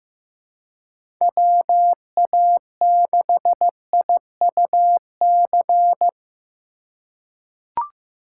Morse code, heard as a keyed tone, about 15 wpm, 700 Hz. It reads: WA6IUC
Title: USA Call Signs - 2x Character Spacing - 15wpm